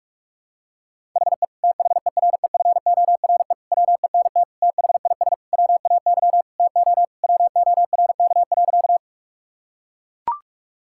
Morse code, heard as a keyed tone, about 42 wpm, 700 Hz. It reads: HE THEREFORE WENT THIS WAY TO WORK.
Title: Sentences from Top 1000 Words - Review Part 5 - 45wpm